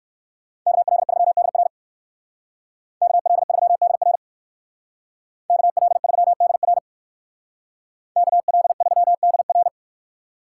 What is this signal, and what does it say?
Morse code, heard as a keyed tone, about 58 wpm, 700 Hz. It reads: KL3DR KL3DR KL3DR KL3DR